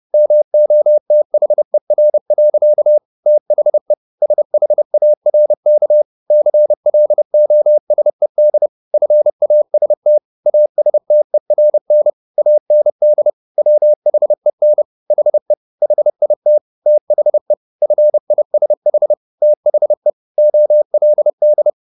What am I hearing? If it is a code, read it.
Morse code, heard as a keyed tone, about 30 wpm, 600 Hz. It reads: MOTHER. THE SHARK CLOSED FAST ASTERN AND WHEN HE HIT THE FISH THE OLD